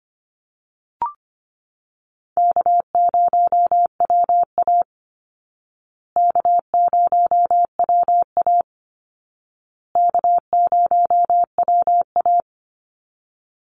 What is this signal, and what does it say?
Morse code, heard as a keyed tone, about 25 wpm, 700 Hz. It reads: K0WA K0WA K0WA
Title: Easy Call Signs Repeated 3x - 25wpm